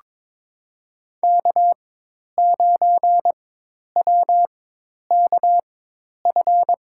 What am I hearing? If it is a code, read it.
Morse code, heard as a keyed tone, about 22 wpm, spaced as for 10 wpm, 700 Hz. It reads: K9WKF
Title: USA Call Signs - 5 Characters - 4x Character Spacing - 22wpm